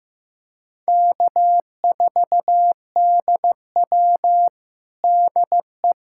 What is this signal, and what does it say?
Morse code, heard as a keyed tone, about 16 wpm, 700 Hz. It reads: K4DW DE
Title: USA Callsign DE Callsign - 4 Characters - 15wpm